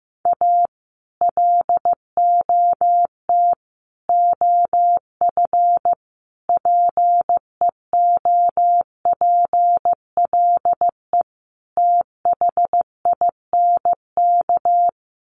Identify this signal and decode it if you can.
Morse code, heard as a keyed tone, about 15 wpm, 700 Hz. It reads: A LOT OF PEOPLE THINK
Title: Sets of 5 Words Encore 15wpm